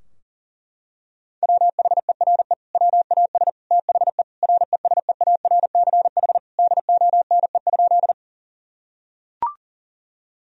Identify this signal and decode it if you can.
Morse code, heard as a keyed tone, about 40 wpm, 700 Hz. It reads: WHERE WAS THE RESEARCH DONE?
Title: Sentences from Top 1000 Words - Review Part 4 - 40wpm